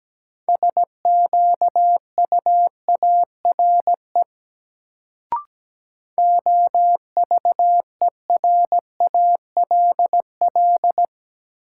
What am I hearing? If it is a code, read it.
Morse code, heard as a keyed tone, about 17 wpm, 700 Hz. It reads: SQUARE OVERALL
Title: Sentences from Top 1000 Words - Review Part 7 - New Word Drill - 17wpm